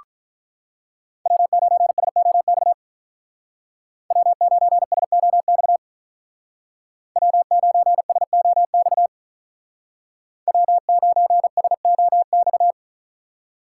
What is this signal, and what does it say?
Morse code, heard as a keyed tone, about 43 wpm, 700 Hz. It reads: W9SOX W9SOX W9SOX W9SOX
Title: Speed Racing - USA 1x3 Call Signs 35wpm